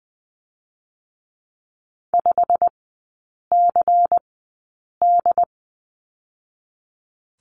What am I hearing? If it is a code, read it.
Morse code, heard as a keyed tone, about 20 wpm, spaced as for 8 wpm, 700 Hz. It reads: 5CD
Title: 184 Lesson 30e - Letter b - Three Characters 20wpm